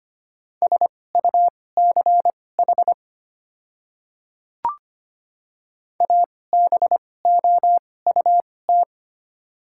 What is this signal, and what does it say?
Morse code, heard as a keyed tone, about 25 wpm, spaced as for 18 wpm, 690 Hz. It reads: SUCH ABOUT
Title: 2x Character Spacing - Top 200 Words - 25wpm